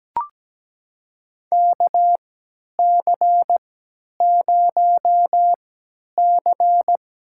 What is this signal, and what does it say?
Morse code, heard as a keyed tone, about 17 wpm, spaced as for 10 wpm, 700 Hz. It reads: KC0C
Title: USA Call Signs - 3x Character Spacing - 17wpm